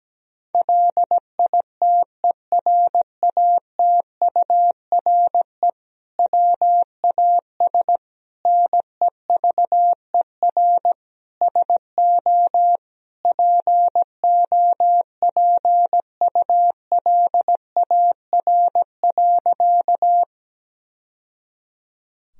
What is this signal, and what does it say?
Morse code, heard as a keyed tone, about 17 wpm, 700 Hz. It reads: LITERATURE WAS NEVER SO POPULAR.
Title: Sentences from Top 1000 Words - Review Part 3 - 17wpm